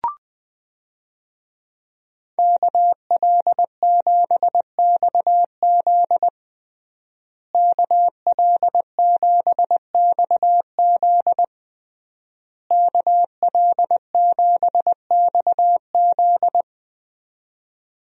Morse code, heard as a keyed tone, about 21 wpm, 700 Hz. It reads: KL7XZ KL7XZ KL7XZ
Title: DXCC-100 Call Signs - 5 Characters - Repeated 3x - 20wpm